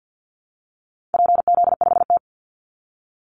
Morse code, heard as a keyed tone, about 49 wpm, 700 Hz. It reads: PZ5T
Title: Call Signs Repeated 2x - 50wpm